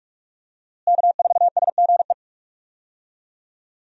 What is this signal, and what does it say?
Morse code, heard as a keyed tone, about 47 wpm, 700 Hz. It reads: K4SGE
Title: USA Call Signs - 5 Characters - Repeated 3x - 45wpm